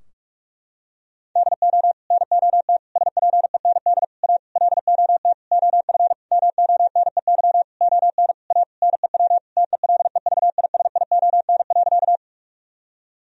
Morse code, heard as a keyed tone, about 41 wpm, 700 Hz. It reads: DO NOT SPEND A LOT OF MONEY ON A NEW TELEVISION.
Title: Sentences from Top 1000 Words - Review Part 4 - 45wpm